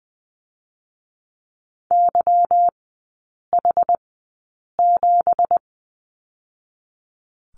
Morse code, heard as a keyed tone, about 20 wpm, spaced as for 8 wpm, 700 Hz. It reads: YH7
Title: 166 Lesson 27e - Number 7 - Three Characters 20wpm